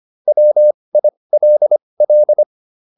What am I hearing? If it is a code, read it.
Morse code, heard as a keyed tone, about 25 wpm, spaced as for 20 wpm, 600 Hz. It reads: WILL
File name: SS209 20